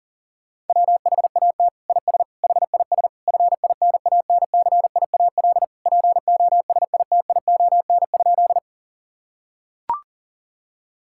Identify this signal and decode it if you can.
Morse code, heard as a keyed tone, about 40 wpm, 700 Hz. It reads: WHAT IS HIS FINANCIAL POSITION?
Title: Sentences from Top 1000 Words - Review Part 5 - 40wpm